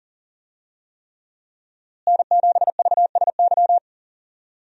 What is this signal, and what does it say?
Morse code, heard as a keyed tone, about 40 wpm, 700 Hz. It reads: N7VSY
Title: USA Call Signs - 5 Characters - Repeated 3x - 40wpm